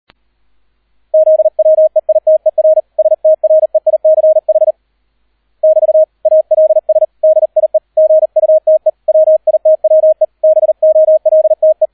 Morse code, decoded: ZWEITER STREICH <BT> ALS DIE GUTE WITWE BOLTE

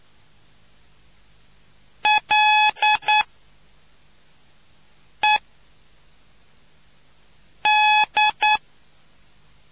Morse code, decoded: L E D

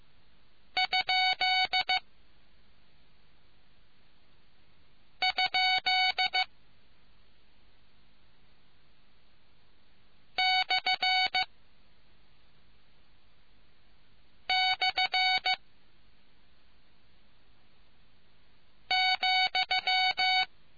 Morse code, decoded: ? ? / / ,

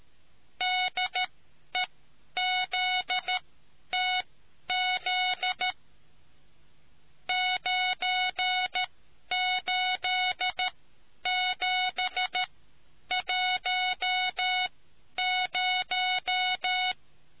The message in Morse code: DEZTZ 98710

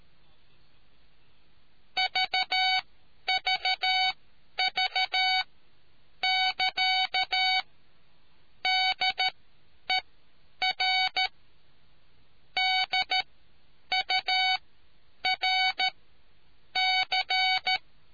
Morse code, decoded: VVV <KA> DER DURC

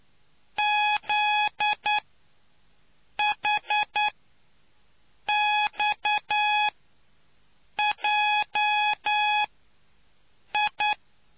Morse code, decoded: ZHXJI

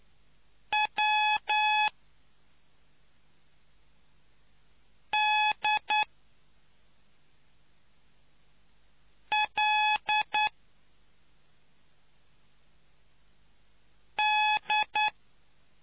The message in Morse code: W D L D

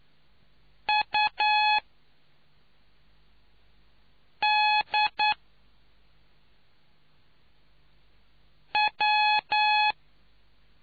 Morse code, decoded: UDW

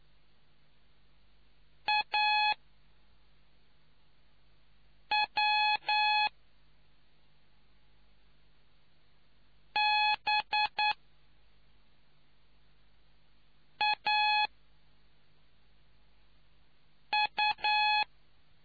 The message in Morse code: AWBAU

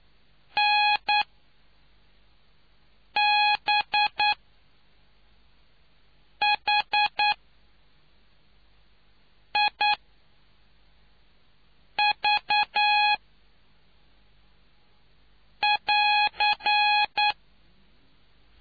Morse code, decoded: NBHIV<AR>

